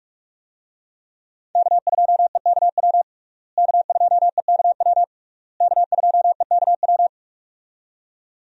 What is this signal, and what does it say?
Morse code, heard as a keyed tone, about 46 wpm, 700 Hz. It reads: K2EKW K2EKW K2EKW
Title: USA Call Signs - 5 Characters - Repeated 3x - 45wpm